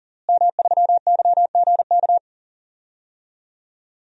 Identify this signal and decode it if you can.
Morse code, heard as a keyed tone, about 40 wpm, 700 Hz. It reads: M3YGK
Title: DXCC-100 Call Signs - 5 Characters - Repeated 2x - 40wpm